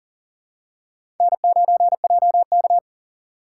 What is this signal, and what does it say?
Morse code, heard as a keyed tone, about 40 wpm, 700 Hz. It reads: N9JK
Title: USA Call Signs - 4 Characters - 2x Character Spacing - 40wpm